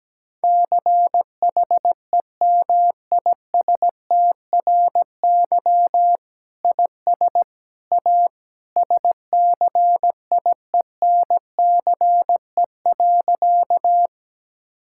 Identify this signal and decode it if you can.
Morse code, heard as a keyed tone, about 17 wpm, 700 Hz. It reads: CHEMISTRY IS A SCIENCE.
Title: Sentences from Top 1000 Words - Review Part 6 - 17wpm